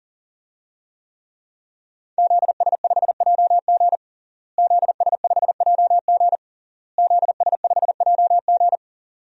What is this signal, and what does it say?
Morse code, heard as a keyed tone, about 40 wpm, 700 Hz. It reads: ZS5JG ZS5JG ZS5JG
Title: DXCC-100 Call Signs - 5 Characters - Repeated 3x - 40wpm